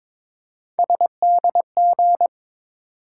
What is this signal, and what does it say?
Morse code, heard as a keyed tone, about 22 wpm, 700 Hz. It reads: SDG
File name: ARRL Sections v2 - 22wpm